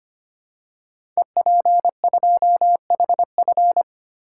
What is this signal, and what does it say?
Morse code, heard as a keyed tone, about 25 wpm, 700 Hz. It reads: EP2HF
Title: Code Groups - 5 Letter:Numbers - 25wpm